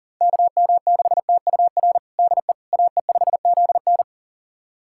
Morse code, heard as a keyed tone, about 40 wpm, 700 Hz. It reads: KM6TUR DE AE5ZN
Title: USA Callsign DE Callsign 40wpm